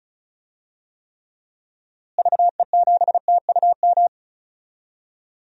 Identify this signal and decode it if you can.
Morse code, heard as a keyed tone, about 34 wpm, 700 Hz. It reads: VE7TUM
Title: DXCC-100 Call Signs - 6 Characters - Repeated 2x - 35wpm